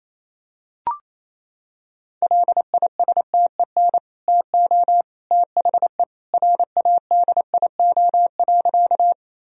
Morse code, heard as a keyed tone, about 27 wpm, 700 Hz. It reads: LISTEN TO THE RADIO.
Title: Sentences from Top 1000 Words - Review Part 3 - 28wpm